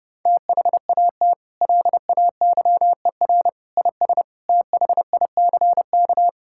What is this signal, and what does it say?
Morse code, heard as a keyed tone, about 32 wpm, 700 Hz. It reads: THAT LAYER IS THICK
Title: Sentences from Top 1600 Words 30wpm